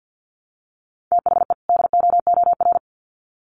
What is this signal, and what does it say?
Morse code, heard as a keyed tone, about 50 wpm, 700 Hz. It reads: THE DOOR
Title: Sets of 2 Words 50wpm